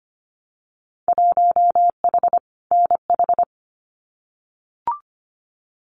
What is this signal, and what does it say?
Morse code, heard as a keyed tone, about 25 wpm, 700 Hz. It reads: 1H NH
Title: Winter Field Day Exchanges 25wpm